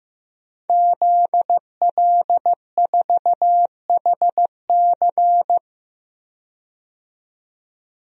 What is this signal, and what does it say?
Morse code, heard as a keyed tone, about 15 wpm, 700 Hz. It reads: ZL4HC
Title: DXCC-100 Call Signs - 5 Characters - 15wpm